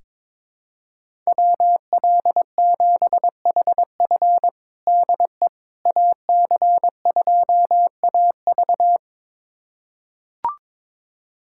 Morse code, heard as a keyed tone, about 22 wpm, 700 Hz. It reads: WL7HF DE AC2AV
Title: USA Callsign DE Callsign - 5 Characters - 22wpm